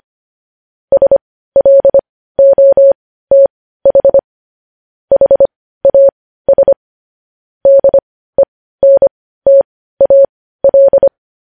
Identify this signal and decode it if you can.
Morse code, heard as a keyed tone, about 25 wpm, spaced as for 15 wpm, 550 Hz. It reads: SLOTH HAS DENTAL
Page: news headlines in morse code at 15 wpm